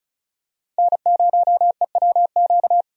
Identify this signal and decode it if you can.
Morse code, heard as a keyed tone, about 36 wpm, 700 Hz. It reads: N0EWQ
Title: USA Call Signs - 5 Characters - 35wpm